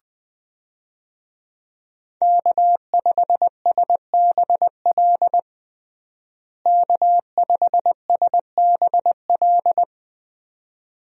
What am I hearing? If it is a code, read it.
Morse code, heard as a keyed tone, about 20 wpm, 700 Hz. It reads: K5SBL K5SBL
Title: USA Call Signs - 5 Characters - Repeated 2x - 20wpm